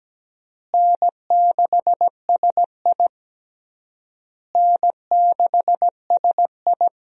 Morse code, heard as a keyed tone, about 17 wpm, 700 Hz. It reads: N6SI N6SI
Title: USA Call Signs - 4 Characters - Repeated 3x - 17wpm